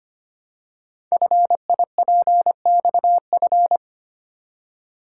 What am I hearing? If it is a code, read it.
Morse code, heard as a keyed tone, about 25 wpm, 700 Hz. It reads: FIPXF
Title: Code Groups - 5 Letters - 25wpm